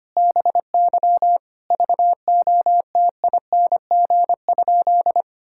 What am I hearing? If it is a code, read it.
Morse code, heard as a keyed tone, about 25 wpm, 700 Hz. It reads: BY VOTING?
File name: practice / 25wpm / Sentences with 9-Word Phrases - 25wpm